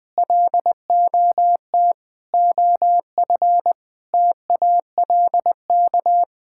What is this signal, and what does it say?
Morse code, heard as a keyed tone, about 20 wpm, 700 Hz. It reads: LOT OF TALK
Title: Sets of 8 Words - 20wpm